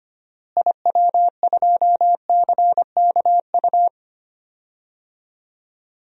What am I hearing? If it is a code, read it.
Morse code, heard as a keyed tone, about 25 wpm, 700 Hz. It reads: IW2CKU
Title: DXCC-100 Call Signs - 6 Characters - Repeated 2x - 25wpm